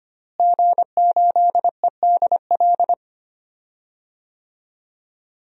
Morse code, heard as a keyed tone, about 25 wpm, 700 Hz. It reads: G8EDL
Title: DXCC-100 Call Signs - 5 Characters - Repeated 2x - 25wpm